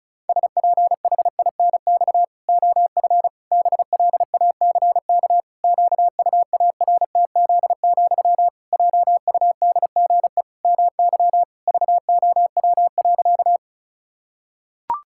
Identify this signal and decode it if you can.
Morse code, heard as a keyed tone, about 36 wpm, 700 Hz. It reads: SPHINX OF BLACK QUARTZ, JUDGE MY VOW.
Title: Pangram Challenge v2 - 35wpm